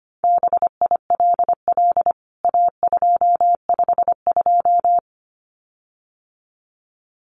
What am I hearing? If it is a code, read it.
Morse code, heard as a keyed tone, about 25 wpm, 700 Hz. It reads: BILL A252
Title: CWT Exchange 25wpm